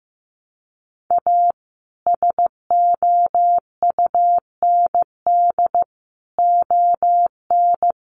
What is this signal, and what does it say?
Morse code, heard as a keyed tone, about 15 wpm, 700 Hz. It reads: A SOUND ON
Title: A Sound Of Thunder 15wpm